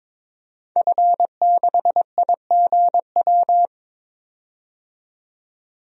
Morse code, heard as a keyed tone, about 22 wpm, 700 Hz. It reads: F6IGW
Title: DXCC-100 Call Signs - 5 Characters - 22wpm